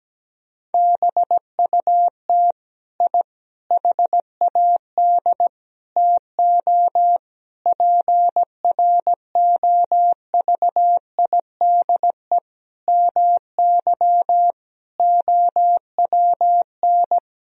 Morse code, heard as a keyed tone, about 17 wpm, 700 Hz. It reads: BUT I HAD TO PROVIDE MY OWN
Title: Sentences from Top 1000 Words - Review Part 3 - 17wpm